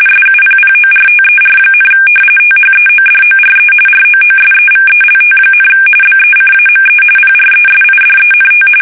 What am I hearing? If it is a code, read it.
RTTY, about 45.45 baud. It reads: SUPPORTER, THEODORE ROOSEVELT, BECAME PRESIDENT
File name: fsk850 / usnavy3